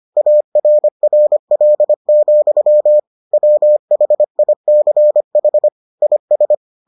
Morse code, heard as a keyed tone, about 25 wpm, 600 Hz. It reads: ARRL, WHICH IS